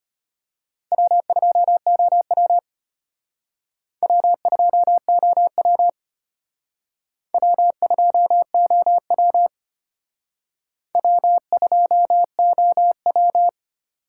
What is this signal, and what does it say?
Morse code, heard as a keyed tone, about 33 wpm, 700 Hz. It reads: W2OW W2OW W2OW W2OW